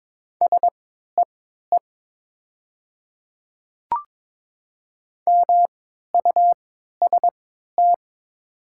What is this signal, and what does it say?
Morse code, heard as a keyed tone, about 22 wpm, spaced as for 13 wpm, 700 Hz. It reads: SEE MUST
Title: 3x Character Spacing - Top 300 Words - 22wpm